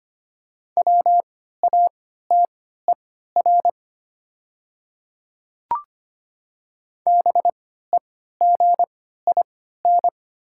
Morse code, heard as a keyed tone, about 25 wpm, spaced as for 14 wpm, 700 Hz. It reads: WATER BEGIN
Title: 3x Character Spacing - Top 300 Words - 25wpm